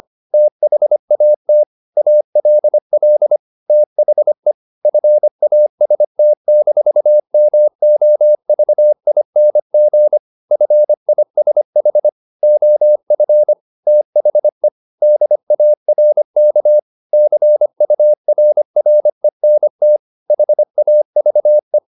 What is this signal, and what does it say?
Morse code, heard as a keyed tone, about 25 wpm, 600 Hz. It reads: THAT ALL THE FAST-MOVING FISH OF THE DARK CURRENT HAVE